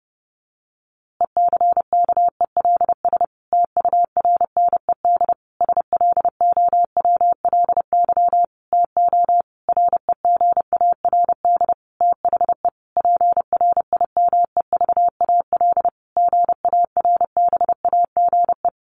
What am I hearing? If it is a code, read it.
Morse code, heard as a keyed tone, about 30 wpm, 700 Hz. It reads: ECKELS TURNED SLOWLY TO REGARD THE PRIMEVAL GARBAGE
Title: A Sound Of Thunder 30wpm